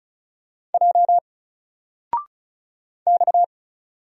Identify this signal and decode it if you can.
Morse code, heard as a keyed tone, about 33 wpm, 700 Hz. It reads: J X
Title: Single Letter-Number Alt 35wpm